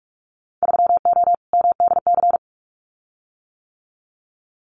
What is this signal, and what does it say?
Morse code, heard as a keyed tone, about 46 wpm, 700 Hz. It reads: 3O MDC
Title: Winter Field Day Exchanges Encore 45wpm